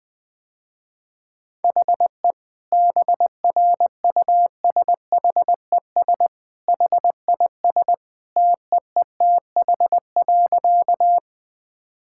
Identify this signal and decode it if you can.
Morse code, heard as a keyed tone, about 20 wpm, 700 Hz. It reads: HE BRUSHES HIS TEETH.
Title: Easy Sentences with Hints - 20wpm